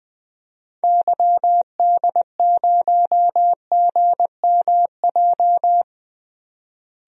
Morse code, heard as a keyed tone, about 20 wpm, 700 Hz. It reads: YD0GMJ
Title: DXCC-100 Call Signs - 6 Characters - Repeated 2x - 20wpm